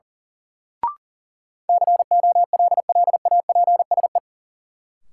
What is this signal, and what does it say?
Morse code, heard as a keyed tone, about 37 wpm, 700 Hz. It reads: COLLAPSE
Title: Sets of 5 Words Encore - New Word Drill - 40wpm